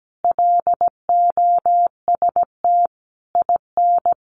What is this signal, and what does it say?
Morse code, heard as a keyed tone, about 17 wpm, 700 Hz. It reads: LOST IN
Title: Sentences from Top 1400 Words 17wpm